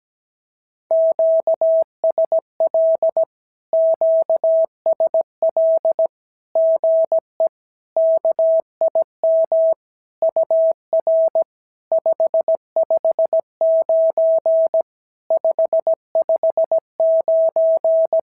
Morse code, heard as a keyed tone, about 17 wpm, 650 Hz. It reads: QSL QSL GE KIM UR 559 559
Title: SOTA QSO Practice 17wpm